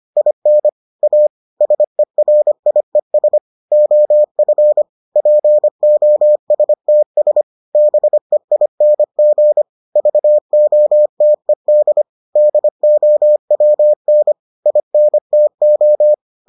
Morse code, heard as a keyed tone, about 25 wpm, 600 Hz. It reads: IN A SERIES OF POSTS BEING VOTED DOWN INTO